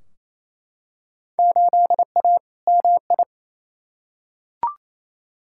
Morse code, heard as a keyed tone, about 28 wpm, 700 Hz. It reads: 8A MI
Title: Field Day Exchanges Encore - 28wpm